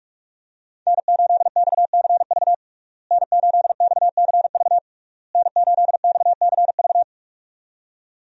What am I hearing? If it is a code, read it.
Morse code, heard as a keyed tone, about 46 wpm, 700 Hz. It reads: N8XCV N8XCV N8XCV
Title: USA Call Signs - 5 Characters - Repeated 3x - 45wpm